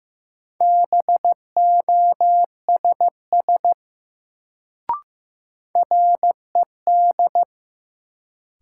Morse code, heard as a keyed tone, about 15 wpm, 700 Hz. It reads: BOSS RED
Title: Sentences from Top 700 Words - New Word Drill - 15wpm